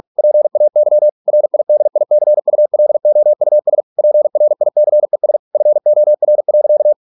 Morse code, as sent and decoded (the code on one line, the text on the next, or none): .--. .- -.-- / .-. .. -.. .. -.-. ..- .-.. --- ..- ... / .--. .-. .. -.-. ... / ..-. --- .-. .-.-.-
PAY RIDICULOUS PRICES FOR.